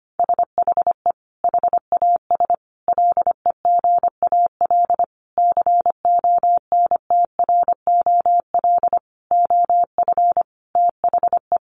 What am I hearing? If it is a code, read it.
Morse code, heard as a keyed tone, about 25 wpm, 700 Hz. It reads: SHE HAS LEGAL CONTROL OF THE
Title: Sentences from Top 1200 Words 25wpm